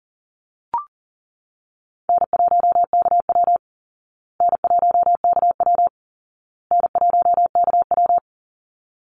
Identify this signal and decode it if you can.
Morse code, heard as a keyed tone, about 40 wpm, 700 Hz. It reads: N1KW N1KW N1KW
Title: Easy Call Signs Repeated 3x - 40wpm